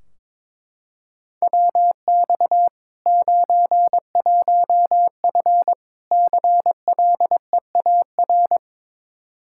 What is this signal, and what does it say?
Morse code, heard as a keyed tone, about 22 wpm, 700 Hz. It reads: WX 91F CLEAR
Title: Practice QSO - Exchange 2 - Info - 22wpm